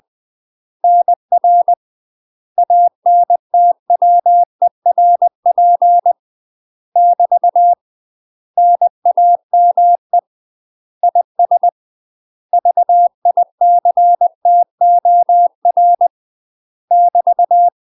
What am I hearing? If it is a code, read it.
Morse code, heard as a keyed tone, about 20 wpm, 700 Hz. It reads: NR ANTWERP <BT> NAME IS VICTOR <BT>
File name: UK MASTER QSOs TRAINING / 20WPM / UK MASTER QSOs 0018